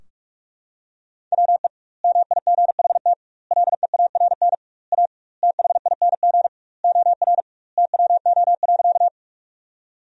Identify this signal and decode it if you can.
Morse code, heard as a keyed tone, about 46 wpm, 700 Hz. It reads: WE MIGHT LEARN A THING OR TWO.
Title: 2x Word-Spacing - Sentences from Top 300 Words - 45wpm